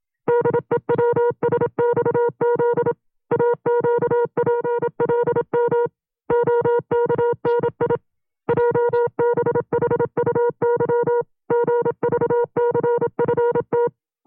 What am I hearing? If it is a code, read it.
Morse code, heard as a keyed tone, about 27 wpm, 480 Hz. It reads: DEWSXZ AQPLM OKNI JBHUY GVCFT
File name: FreeFiles / CW-practice / 25WPM-side2